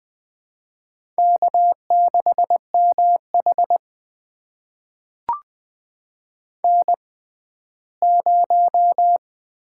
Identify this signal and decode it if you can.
Morse code, heard as a keyed tone, about 20 wpm, spaced as for 14 wpm, 700 Hz. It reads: K6MH N0
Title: USA Call Signs - 4 Characters - 6x Character Spacing - 20wpm